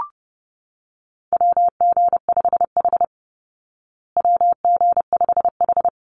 Morse code, heard as a keyed tone, about 30 wpm, 700 Hz. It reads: WG5H WG5H
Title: Call Signs Repeated 3x - 30wpm